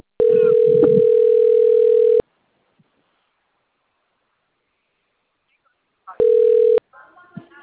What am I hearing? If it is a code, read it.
Ringback tone (North America), 440 + 480 Hz, with speech.
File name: out- - - - - .9721